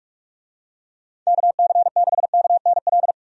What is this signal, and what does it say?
Morse code, heard as a keyed tone, about 46 wpm, 700 Hz. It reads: KC6KNL
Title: USA Call Signs - 6 Characters - 45wpm